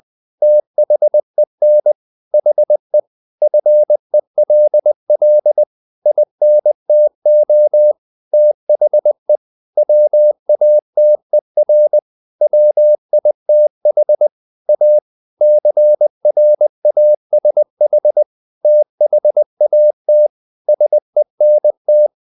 Morse code, heard as a keyed tone, about 20 wpm, 600 Hz. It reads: THEN HE FELL INTO THE WATER WITH A CRASH THAT SENT
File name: the old man and the sea part 2 20wpm